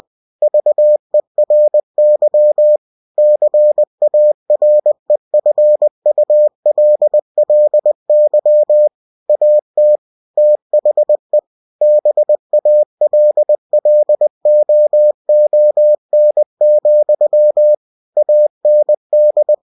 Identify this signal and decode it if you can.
Morse code, heard as a keyed tone, about 20 wpm, 600 Hz. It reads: VERY CAREFULLY AT THE BALLOON, AND